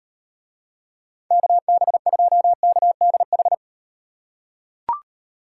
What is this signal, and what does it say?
Morse code, heard as a keyed tone, about 38 wpm, 700 Hz. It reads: KB2KDH